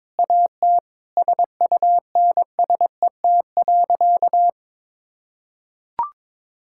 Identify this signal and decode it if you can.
Morse code, heard as a keyed tone, about 22 wpm, 700 Hz. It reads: AT SUNSET.
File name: Sentences with 7-Word Phrases - 22wpm